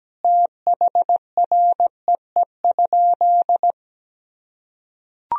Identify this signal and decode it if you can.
Morse code, heard as a keyed tone, about 17 wpm, 700 Hz. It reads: THREE?
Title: Sentences from Top 1000 Words - Review Part 2 - 17wpm